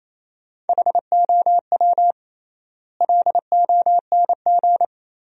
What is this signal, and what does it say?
Morse code, heard as a keyed tone, about 28 wpm, 700 Hz. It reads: HOW LONG
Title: 3x Word-Spacing - Sentences from Top 500 Words - 28wpm